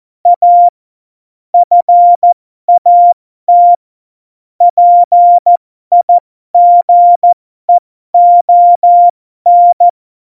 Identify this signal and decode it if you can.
Morse code, heard as a keyed tone, about 14 wpm, spaced as for 12 wpm, 700 Hz. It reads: A FAT PIGEON
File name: Morsekurs / Witze